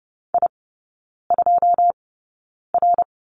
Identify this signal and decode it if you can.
Morse code, heard as a keyed tone, about 30 wpm, spaced as for 9 wpm, 700 Hz. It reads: I2R
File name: 081 Lesson 14e - Number 2 - Three Characters 30wpm